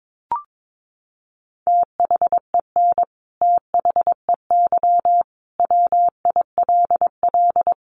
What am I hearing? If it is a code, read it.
Morse code, heard as a keyed tone, about 22 wpm, 700 Hz. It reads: THEN THEY WILL
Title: Sentences from Top 200 Words 22wpm